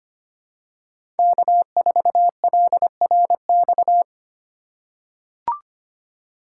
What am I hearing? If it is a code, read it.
Morse code, heard as a keyed tone, about 25 wpm, 700 Hz. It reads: K4LRX
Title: K1USN SST QSO Practice - 25wpm